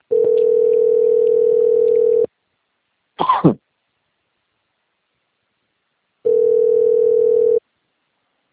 Ringback tone (North America), 440 + 480 Hz.